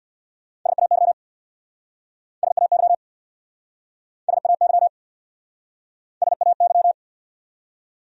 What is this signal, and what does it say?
Morse code, heard as a keyed tone, about 68 wpm, 700 Hz. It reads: SAY SAY SAY SAY